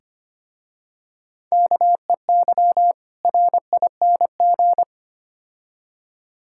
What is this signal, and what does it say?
Morse code, heard as a keyed tone, about 25 wpm, 700 Hz. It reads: KEY RING
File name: practice / 25wpm / Difficult Words with Hints - 25wpm